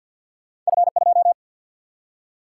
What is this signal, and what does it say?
Morse code, heard as a keyed tone, about 52 wpm, 700 Hz. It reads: F2
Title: Single-Letter Single-Number 50wpm